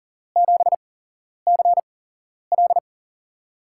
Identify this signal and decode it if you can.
Morse code, heard as a keyed tone, about 39 wpm, spaced as for 11 wpm, 700 Hz. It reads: 7CL
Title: USA Call Signs - 5 Characters - 8x Character Spacing - 40wpm